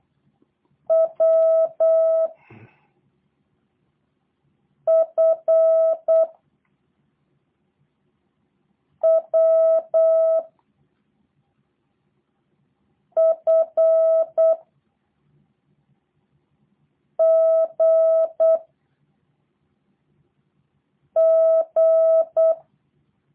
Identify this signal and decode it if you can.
Morse code, heard as a keyed tone, about 8 wpm, 650 Hz. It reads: W F W F G G
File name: jh8cbh / 2023cw / review1-4